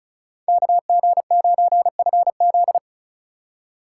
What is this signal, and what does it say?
Morse code, heard as a keyed tone, about 46 wpm, 700 Hz. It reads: KG9FZ